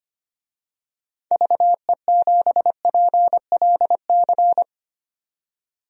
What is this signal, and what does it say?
Morse code, heard as a keyed tone, about 25 wpm, 700 Hz. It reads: VE7PLC
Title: DXCC-100 Call Signs - 6 Characters - 25wpm